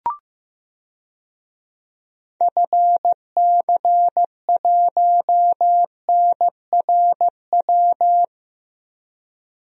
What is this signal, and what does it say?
Morse code, heard as a keyed tone, about 15 wpm, 700 Hz. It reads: FC1NRW
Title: DXCC-100 Call Signs - 6 Characters - Repeated 3x - 15wpm